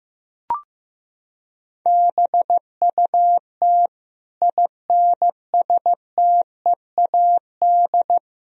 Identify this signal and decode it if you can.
Morse code, heard as a keyed tone, about 15 wpm, 700 Hz. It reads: BUT INSTEAD
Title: Sentences from Top 1000 Words - Review Part 4 - 15wpm